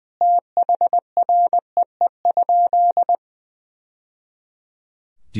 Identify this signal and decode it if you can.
Morse code, heard as a keyed tone, about 20 wpm, 700 Hz. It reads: THREE?
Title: Sentences from Top 1000 Words - Review Part 2 - 20wpm